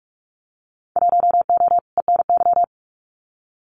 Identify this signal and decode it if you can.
Morse code, heard as a keyed tone, about 46 wpm, 700 Hz. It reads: 1O ENY
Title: Winter Field Day Exchanges 45wpm